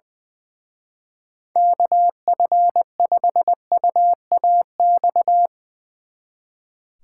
Morse code, heard as a keyed tone, about 22 wpm, 700 Hz. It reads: KF5UAX
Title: Speed Racing - USA 2x3 Call Signs 20wpm